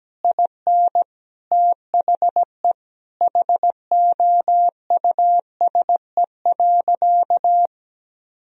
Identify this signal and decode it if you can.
Morse code, heard as a keyed tone, about 17 wpm, 700 Hz. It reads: IN THE HOUSE.
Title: Sentences from Top 1000 Words - Review Part 4 - 17wpm